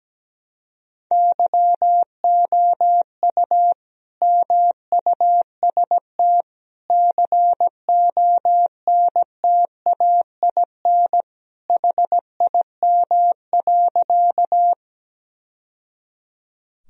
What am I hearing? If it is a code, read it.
Morse code, heard as a keyed tone, about 17 wpm, 700 Hz. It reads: YOU MUST CONTAIN HIM.
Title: Sentences from Top 1000 Words - Review Part 2 - 17wpm